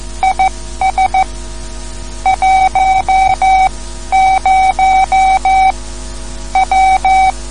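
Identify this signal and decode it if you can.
Morse code, heard as a keyed tone, about 14 wpm, spaced as for 12 wpm, 760 Hz. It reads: IS 10 W
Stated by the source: Morse code lesson, third lesson in a series for learning the Morse alphabet.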